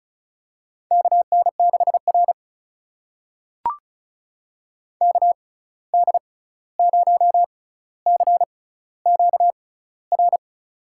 Morse code, heard as a keyed tone, about 36 wpm, spaced as for 16 wpm, 700 Hz. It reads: KN6R KD0CQR
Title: USA Call Signs - 6x Character Spacing - 35wpm